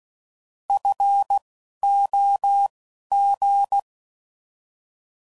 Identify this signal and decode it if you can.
Morse code, heard as a keyed tone, about 16 wpm, spaced as for 11 wpm, 800 Hz. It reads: FOG